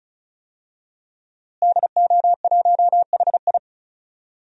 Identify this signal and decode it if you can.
Morse code, heard as a keyed tone, about 34 wpm, 700 Hz. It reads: DO1HI
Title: DXCC-100 Call Signs - 5 Characters - Repeated 2x - 35wpm